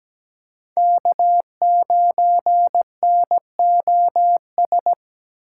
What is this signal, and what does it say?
Morse code, heard as a keyed tone, about 20 wpm, 700 Hz. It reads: K9NOS
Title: Speed Racing - USA 1x3 Call Signs 17wpm